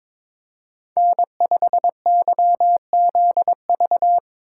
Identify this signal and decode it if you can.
Morse code, heard as a keyed tone, about 24 wpm, 700 Hz. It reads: N5YZV
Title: Speed Racing - USA 1x3 Call Signs 22wpm